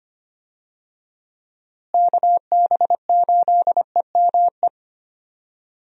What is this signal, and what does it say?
Morse code, heard as a keyed tone, about 25 wpm, 700 Hz. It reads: KB8EME
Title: USA Call Signs - 6 Characters - Repeated 3x - 25wpm